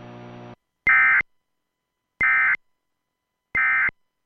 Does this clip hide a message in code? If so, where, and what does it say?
Yes: Emergency Alert System (SAME) tones from 0.8 to 3.9 s, heard as harsh buzzing bursts.